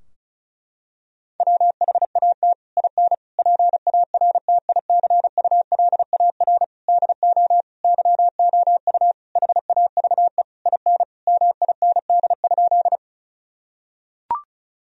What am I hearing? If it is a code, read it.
Morse code, heard as a keyed tone, about 34 wpm, 700 Hz. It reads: WHAT IN PARTICULAR DO YOU HAVE IN MIND?
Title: Sentences from Top 1000 Words - Review Part 4 - 35wpm